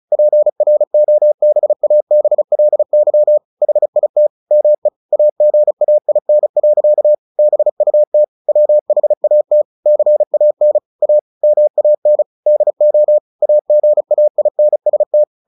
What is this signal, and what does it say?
Morse code, heard as a keyed tone, about 34 wpm, 600 Hz. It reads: PROBABLY HIT ME AGAIN. BUT WHAT CAN A MAN DO AGAINST
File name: the old man and the sea 2 35wpm